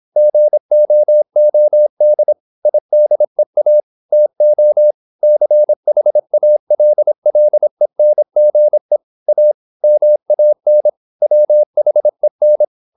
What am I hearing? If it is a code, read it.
Morse code, heard as a keyed tone, about 26 wpm, 600 Hz. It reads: GOOD IDEA TO CHALLENGE A MAN WHEN
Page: Never 26wpm | CW med Gnister